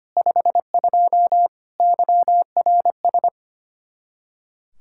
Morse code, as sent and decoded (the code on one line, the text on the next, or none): ..... ..--- / -.-- .-. ...
52 YRS